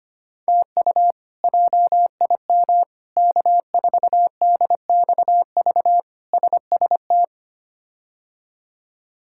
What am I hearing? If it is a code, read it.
Morse code, heard as a keyed tone, about 25 wpm, 700 Hz. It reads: TU JIM K4DXV SST
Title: K1USN SST QSO Practice - 25wpm